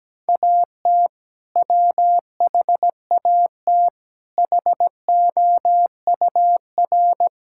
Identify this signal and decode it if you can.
Morse code, heard as a keyed tone, about 17 wpm, 700 Hz. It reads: AT WHAT HOUR
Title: Sentences from Top 1600 Words 17wpm